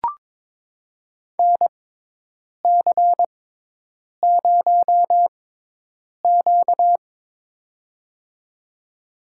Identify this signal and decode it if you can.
Morse code, heard as a keyed tone, about 22 wpm, spaced as for 8 wpm, 700 Hz. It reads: NC0Q
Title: USA Call Signs - 4 Characters - 6x Character Spacing - 22wpm